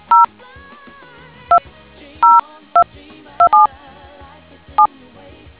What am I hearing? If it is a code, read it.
DTMF, heard as telephone touch tones, tones of about 130 ms. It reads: 0202277